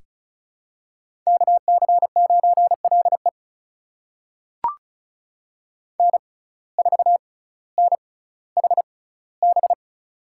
Morse code, heard as a keyed tone, about 36 wpm, spaced as for 18 wpm, 700 Hz. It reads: KC9LE N4NHB
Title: USA Call Signs - 6x Character Spacing - 35wpm